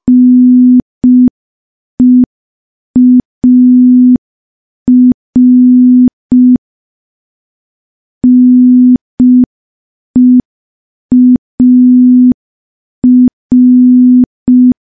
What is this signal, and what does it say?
Morse code, heard as a keyed tone, about 5 wpm, 250 Hz. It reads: NEAR NEAR
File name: morse / morse-training-05WPM